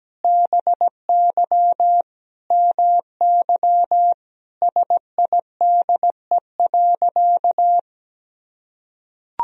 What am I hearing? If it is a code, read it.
Morse code, heard as a keyed tone, about 17 wpm, 700 Hz. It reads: BY MY SIDE.
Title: Sentences from Top 1000 Words - Review Part 1 - 17wpm